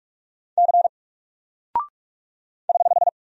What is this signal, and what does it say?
Morse code, heard as a keyed tone, about 44 wpm, 700 Hz. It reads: C <HH>
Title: Single Character (Letter:Number:Punctuation:Prosign) 45wpm